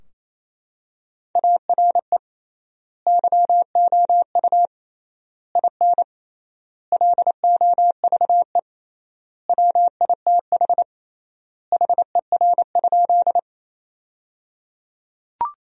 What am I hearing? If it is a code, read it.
Morse code, heard as a keyed tone, about 28 wpm, 700 Hz. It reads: ARE YOU IN LOVE WITH HER?
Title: 3x Word-Spacing - Sentences from Top 400 Words - 28wpm